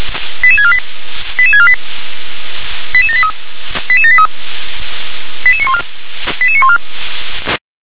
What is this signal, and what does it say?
Five-tone selective call (ZVEI), heard as quick runs of five tones, tones of about 70 ms. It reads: 88648 88648 88783 88783 88914 88914